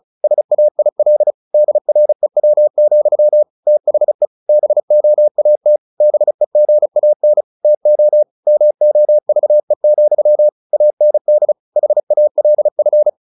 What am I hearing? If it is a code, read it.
Morse code, heard as a keyed tone, about 34 wpm, 600 Hz. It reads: SAIL DREW, THE BOAT BEGAN TO MOVE, AND HALF
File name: the old man and the sea 2 35wpm